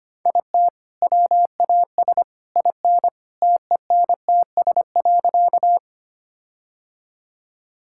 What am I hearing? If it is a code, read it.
Morse code, heard as a keyed tone, about 25 wpm, 700 Hz. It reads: IT WAS IN TENTS.
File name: practice / 25wpm / All-Age Appropriate Jokes - 25wpm